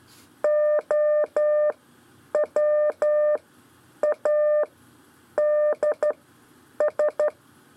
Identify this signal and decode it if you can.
Morse code, heard as a keyed tone, about 12 wpm, spaced as for 8 wpm, 600 Hz. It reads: OWADS
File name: 4 LEZIONE CW 1